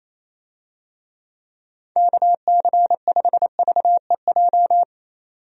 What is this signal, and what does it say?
Morse code, heard as a keyed tone, about 27 wpm, 700 Hz. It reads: KC5VEJ
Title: USA Call Signs - 6 Characters - 28wpm